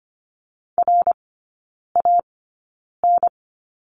Morse code, heard as a keyed tone, about 25 wpm, spaced as for 9 wpm, 700 Hz. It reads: RAN
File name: course / 25:10wpm / 093 Lesson 16e - Letter c - Three Characters 25wpm